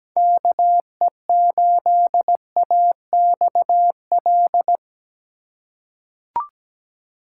Morse code, heard as a keyed tone, about 17 wpm, 700 Hz. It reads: KE8AXL
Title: USA Call Signs - 6 Characters - 4x Character Spacing - 17wpm